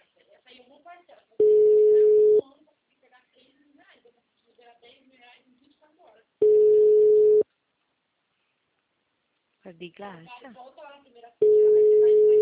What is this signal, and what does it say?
Ringback tone (Europe, 425 Hz), with speech.
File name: out- -2016- - - .26871